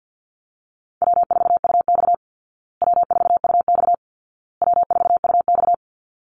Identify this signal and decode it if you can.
Morse code, heard as a keyed tone, about 49 wpm, 700 Hz. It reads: W4UX W4UX W4UX
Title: Easy Call Signs Repeated 3x - 50wpm